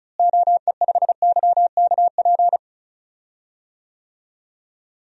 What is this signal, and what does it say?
Morse code, heard as a keyed tone, about 34 wpm, 700 Hz. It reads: OE5YKP
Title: DXCC-100 Call Signs - 6 Characters - Repeated 2x - 35wpm